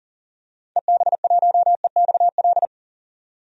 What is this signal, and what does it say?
Morse code, heard as a keyed tone, about 40 wpm, 700 Hz. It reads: EB1EXL
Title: DXCC-100 Call Signs - 6 Characters - 40wpm